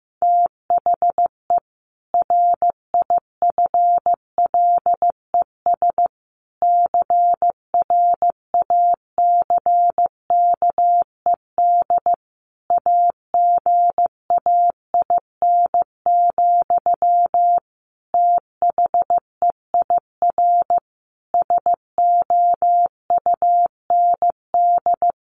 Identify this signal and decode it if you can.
Morse code, heard as a keyed tone, about 14 wpm, 700 Hz. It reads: THE RIFLES CRACKED AGAIN, THEIR SOUND
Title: A Sound Of Thunder 15wpm